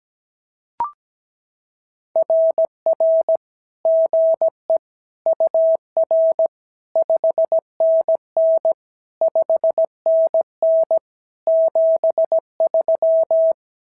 Morse code, heard as a keyed tone, about 17 wpm, 650 Hz. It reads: RR GE UR 5NN 5NN 73
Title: SOTA QSO Practice 17wpm